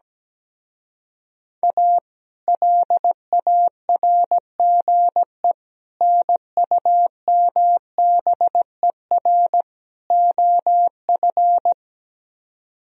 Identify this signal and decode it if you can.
Morse code, heard as a keyed tone, about 17 wpm, 700 Hz. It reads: A LARGE NUMBER OF
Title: Speed-Racing - Sets of 4 Words 17wpm